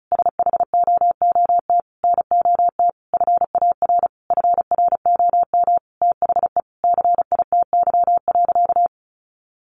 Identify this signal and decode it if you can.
Morse code, heard as a keyed tone, about 34 wpm, 700 Hz. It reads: SHOOT NOT FAR FROM THE CITY.
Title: Sentences from Top 1200 Words 35wpm